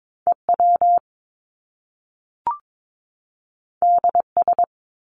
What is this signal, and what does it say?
Morse code, heard as a keyed tone, about 22 wpm, 700 Hz. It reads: EW DS
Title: Top Two Letter Combinations in Words 22wpm